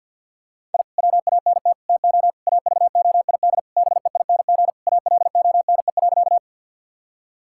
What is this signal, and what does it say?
Morse code, heard as a keyed tone, about 50 wpm, 700 Hz. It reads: I WANT TO AVOID BEING ALONE.
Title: Sentences from Top 1000 Words - Review Part 6 - 50wpm